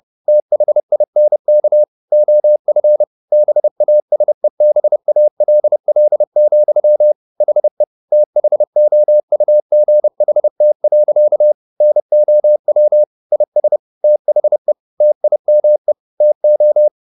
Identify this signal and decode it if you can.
Morse code, heard as a keyed tone, about 30 wpm, 600 Hz. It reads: THINK OF BASEBALL, HE THOUGHT. NOW IS THE TIME TO
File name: the old man and the sea / the old man and the sea 1 30wpm